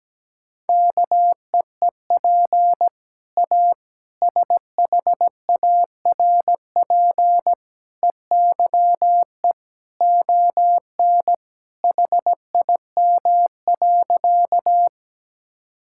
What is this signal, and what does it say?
Morse code, heard as a keyed tone, about 17 wpm, 700 Hz. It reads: KEEP A SHARP EYE ON HIM.
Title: Sentences from Top 1600 Words 17wpm